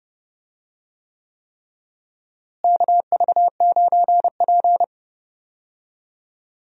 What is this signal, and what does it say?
Morse code, heard as a keyed tone, about 30 wpm, 700 Hz. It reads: KV9P
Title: USA Call Signs - 4 Characters - Repeated 3x - 30wpm